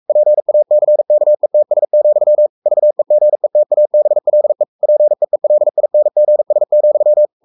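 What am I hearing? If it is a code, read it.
Morse code, heard as a keyed tone, about 44 wpm, 600 Hz. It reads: PACKETS, VEGETABLE PEELINGS,
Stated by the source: Morse code at 43 wpm sending Ken Follett's novel Never in English, from CW med Gnister.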